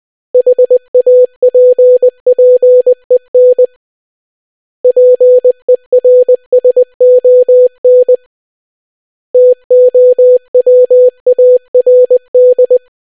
Morse code, heard as a keyed tone, about 20 wpm, 500 Hz. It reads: HAPPEN PERSON TOWARD